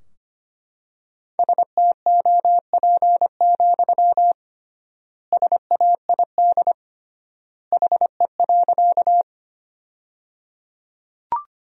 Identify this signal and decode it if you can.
Morse code, heard as a keyed tone, about 25 wpm, 700 Hz. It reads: STOP, SAID HE.
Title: 3x Word-Spacing - Sentences from Top 300 Words - 25wpm